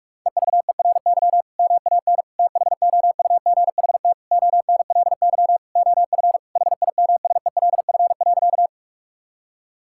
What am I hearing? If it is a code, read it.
Morse code, heard as a keyed tone, about 46 wpm, 700 Hz. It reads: EVERY MAN THOUGHT ONLY OF HIMSELF.
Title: Sentences from Top 1600 Words 45wpm